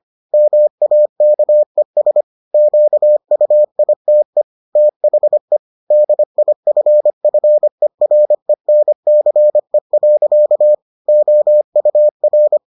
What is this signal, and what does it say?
Morse code, heard as a keyed tone, about 25 wpm, 600 Hz. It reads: MAKES QUITE THE DIFFERENCE. OUR